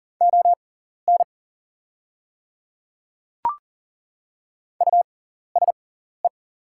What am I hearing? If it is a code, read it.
Morse code, heard as a keyed tone, about 39 wpm, spaced as for 14 wpm, 700 Hz. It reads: ON USE
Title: 6x Character Spacing - Top 100 Words - 40wpm